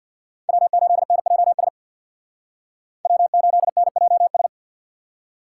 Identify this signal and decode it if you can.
Morse code, heard as a keyed tone, about 59 wpm, 700 Hz. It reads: W8NJS W8NJS
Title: Speed Racing - USA 1x3 Call Signs 50wpm